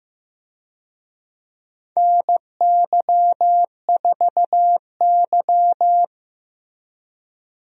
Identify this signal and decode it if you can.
Morse code, heard as a keyed tone, about 15 wpm, 700 Hz. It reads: NY4Y
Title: USA Call Signs - 4 Characters - Repeated 3x - 15wpm